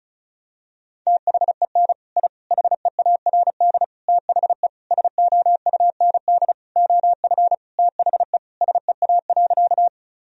Morse code, heard as a keyed tone, about 36 wpm, 700 Hz. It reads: THEN I HEARD THE SOUND OF THE SEA.
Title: Sentences from Top 1000 Words - Review Part 1 - 35wpm